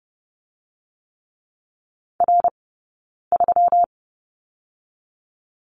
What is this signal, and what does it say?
Morse code, heard as a keyed tone, about 30 wpm, spaced as for 9 wpm, 700 Hz. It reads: R3
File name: Lesson 20d - Number 3 - Two Characters 30wpm